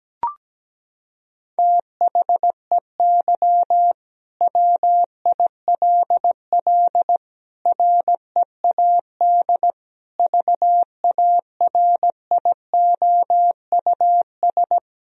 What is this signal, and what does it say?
Morse code, heard as a keyed tone, about 17 wpm, 700 Hz. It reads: THEY WILL READ VARIOUS
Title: Sentences from Top 1000 Words - Review Part 4 - 17wpm